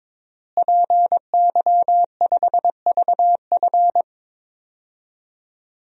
Morse code, heard as a keyed tone, about 22 wpm, 700 Hz. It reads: PY5VF
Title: DXCC-100 Call Signs - 5 Characters - Repeated 3x - 22wpm